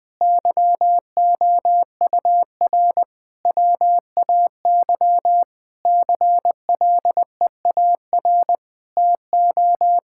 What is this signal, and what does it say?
Morse code, heard as a keyed tone, about 21 wpm, 700 Hz. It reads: YOUR WAY CLEAR TO
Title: Sentences with 9-Word Phrases - 20wpm